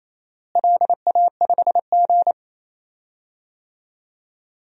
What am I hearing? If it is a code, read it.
Morse code, heard as a keyed tone, about 28 wpm, 700 Hz. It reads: LA5G
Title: DXCC-100 Call Signs - 4 Characters - Repeated 3x - 28wpm